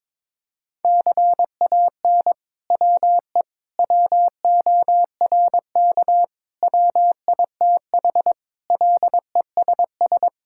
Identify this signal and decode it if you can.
Morse code, heard as a keyed tone, about 22 wpm, 700 Hz. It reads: CAN WE WORK WITH LESS
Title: Sentences from Top 1000 Words - Review Part 5 - 22wpm